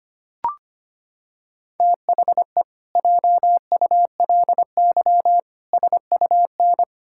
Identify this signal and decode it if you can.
Morse code, heard as a keyed tone, about 25 wpm, 700 Hz. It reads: THE JULY SUN
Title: Pangram Challenge v2 - 25wpm